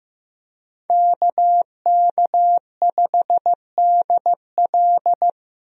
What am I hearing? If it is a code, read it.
Morse code, heard as a keyed tone, about 15 wpm, 700 Hz. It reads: KK5DL
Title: USA Call Signs - 5 Characters - Repeated 3x - 15wpm